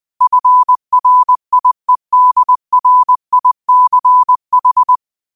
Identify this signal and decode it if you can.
Morse code, heard as a keyed tone, about 20 wpm, 1000 Hz. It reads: FRIEDRICH
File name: qotdfiles / 20wpm